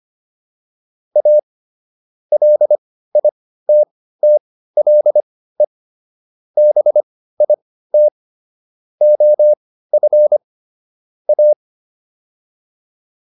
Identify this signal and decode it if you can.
Morse code, heard as a keyed tone, about 25 wpm, spaced as for 15 wpm, 600 Hz. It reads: A LITTLE BIT OF A